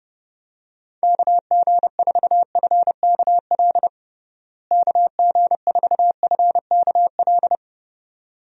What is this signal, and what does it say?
Morse code, heard as a keyed tone, about 30 wpm, 700 Hz. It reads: KG4FKL KG4FKL
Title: USA Call Signs - 6 Characters - Repeated 2x - 30wpm